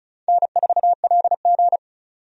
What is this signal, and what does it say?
Morse code, heard as a keyed tone, about 34 wpm, 700 Hz. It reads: N4LG
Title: USA Call Signs - 4 Characters - 35wpm